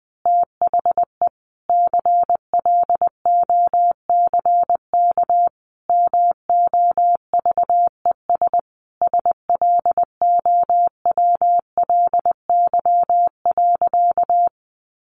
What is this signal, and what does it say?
Morse code, heard as a keyed tone, about 20 wpm, 700 Hz. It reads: THE CLOCK MOVES SLOWLY.
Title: Easy Phrases 20wpm